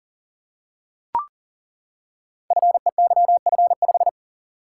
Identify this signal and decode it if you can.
Morse code, heard as a keyed tone, about 38 wpm, 700 Hz. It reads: FEYF5
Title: Code Groups - 5 Letter:Numbers - 40wpm